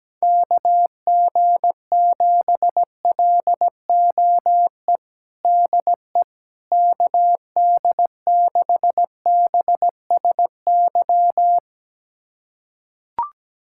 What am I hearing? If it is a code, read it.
Morse code, heard as a keyed tone, about 17 wpm, 700 Hz. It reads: KG7LOE DE KD6BSY
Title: USA Callsign DE Callsign - 6 Characters - 17wpm